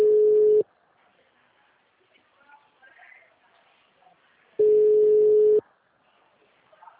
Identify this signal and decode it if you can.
Ringback tone (Europe, 425 Hz).
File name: out- -2005- - - .38